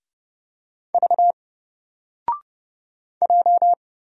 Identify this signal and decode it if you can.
Morse code, heard as a keyed tone, about 31 wpm, 700 Hz. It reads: V J
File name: Single-Letter-Number-30wpm